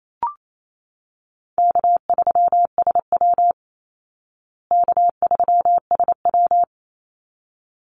Morse code, heard as a keyed tone, about 28 wpm, 700 Hz. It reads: K3SW K3SW
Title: Call Signs Repeated 3x - 28wpm